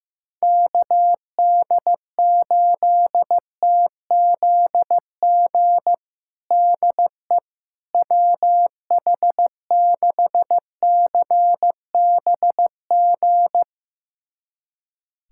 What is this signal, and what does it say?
Morse code, heard as a keyed tone, about 16 wpm, 700 Hz. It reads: KD8TZG DE WH6CBG
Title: USA Callsign DE Callsign - 6 Characters - 15wpm